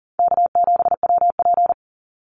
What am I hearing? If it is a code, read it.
Morse code, heard as a keyed tone, about 40 wpm, 700 Hz. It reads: K7WP
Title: Easy Call Signs Repeated 3x - 40wpm